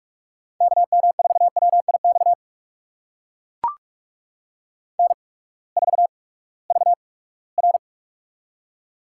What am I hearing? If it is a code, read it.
Morse code, heard as a keyed tone, about 44 wpm, spaced as for 22 wpm, 700 Hz. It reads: KM4WIX N4VR
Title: USA Call Signs - 8x Character Spacing - 45wpm